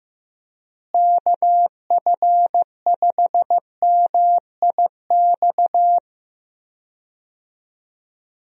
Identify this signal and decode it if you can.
Morse code, heard as a keyed tone, about 15 wpm, 700 Hz. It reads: KF5MIX
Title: USA Call Signs - 6 Characters - Repeated 3x - 15wpm